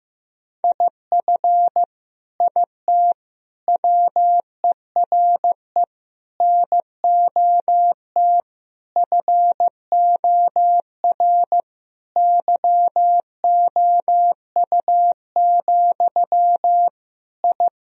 Morse code, heard as a keyed tone, about 15 wpm, 700 Hz. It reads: IF IT WERE NOT FOR YOU, I
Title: Sentences from Top 1000 Words - Review Part 2 - 15wpm